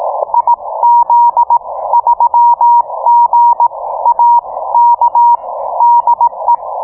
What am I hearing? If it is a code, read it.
Morse code, heard as a keyed tone, about 18 wpm, spaced as for 15 wpm, 950 Hz. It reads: IZ3GAK DE